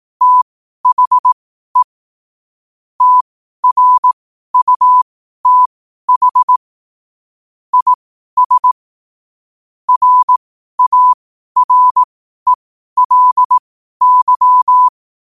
Morse code, heard as a keyed tone, about 18 wpm, spaced as for 12 wpm, 1000 Hz. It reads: THE TRUTH IS RARELY
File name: qotdfiles / 12wpm